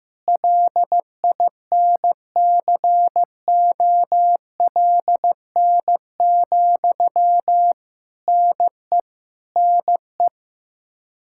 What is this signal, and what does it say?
Morse code, heard as a keyed tone, about 15 wpm, 700 Hz. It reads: LINCOLN, NE NE
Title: Practice QSO - Exchange 1 - Beginning - 15wpm